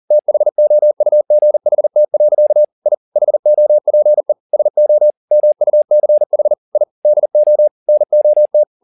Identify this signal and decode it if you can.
Morse code, heard as a keyed tone, about 40 wpm, 600 Hz. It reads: THOUGHT. I HOPE SO MUCH I DO NOT
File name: the old man and the sea 2 40wpm